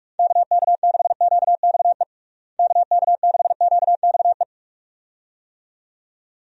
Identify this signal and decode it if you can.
Morse code, heard as a keyed tone, about 46 wpm, 700 Hz. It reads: KK6QXE KK6QXE
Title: USA Call Signs - 6 Characters - Repeated 2x - 45wpm